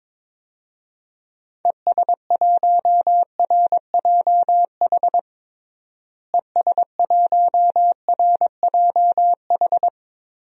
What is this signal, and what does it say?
Morse code, heard as a keyed tone, about 22 wpm, 700 Hz. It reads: ES1RJH ES1RJH
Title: DXCC-100 Call Signs - 6 Characters - Repeated 2x - 22wpm